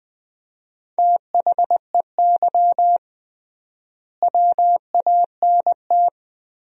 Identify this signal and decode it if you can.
Morse code, heard as a keyed tone, about 20 wpm, 700 Hz. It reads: THEY WANT
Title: 3x Word-Spacing - Sentences from Top 200 Words - 20wpm